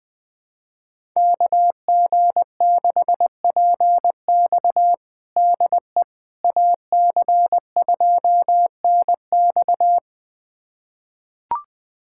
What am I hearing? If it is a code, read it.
Morse code, heard as a keyed tone, about 20 wpm, 700 Hz. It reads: KG6PX DE AC2NX
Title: USA Callsign DE Callsign - 5 Characters - 20wpm